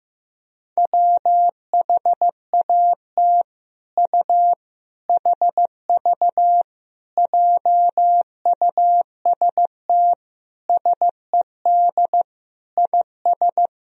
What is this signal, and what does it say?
Morse code, heard as a keyed tone, about 15 wpm, 700 Hz. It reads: WHAT U HV JUST SED IS
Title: Sentences from Top 200 Words and Abbreviations 15wpm